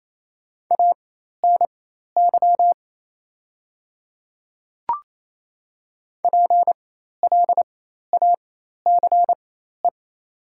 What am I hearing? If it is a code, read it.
Morse code, heard as a keyed tone, about 27 wpm, spaced as for 13 wpm, 700 Hz. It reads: ANY PLACE
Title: 4x Character Spacing - Top 200 Words - 28wpm